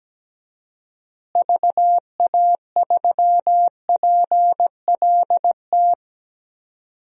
Morse code, heard as a keyed tone, about 17 wpm, 700 Hz. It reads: VA3PLT
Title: DXCC-100 Call Signs - 6 Characters - Repeated 3x - 17wpm